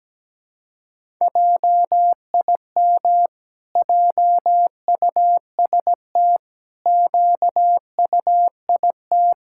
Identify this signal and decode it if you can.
Morse code, heard as a keyed tone, about 17 wpm, 700 Hz. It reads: JIM JUST QUIT
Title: Pangram Challenge v2 - 17wpm